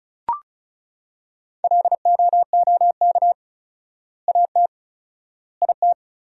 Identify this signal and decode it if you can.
Morse code, heard as a keyed tone, about 34 wpm, 700 Hz. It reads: LOOK AT IT
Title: 4x Word-Spacing - Sentences from Top 200 Words - 35wpm